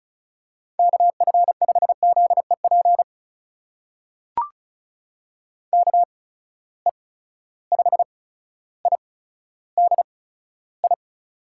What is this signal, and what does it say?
Morse code, heard as a keyed tone, about 36 wpm, spaced as for 15 wpm, 700 Hz. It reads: KF5ZEP KE5IDI
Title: USA Call Signs - 6 Characters - 8x Character Spacing - 35wpm